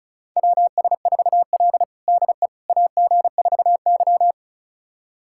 Morse code, heard as a keyed tone, about 36 wpm, 700 Hz. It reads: WS4L DE AG4Y
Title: USA Callsign DE Callsign - 4 Characters - 35wpm